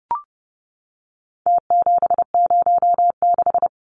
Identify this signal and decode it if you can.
Morse code, heard as a keyed tone, about 30 wpm, 700 Hz. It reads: T706
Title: Numbers - Contest Style - 4-Digit 30wpm